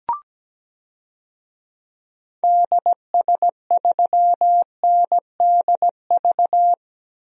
Morse code, heard as a keyed tone, about 17 wpm, 700 Hz. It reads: DS3NDV
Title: DXCC-100 Call Signs - 6 Characters - Repeated 2x - 17wpm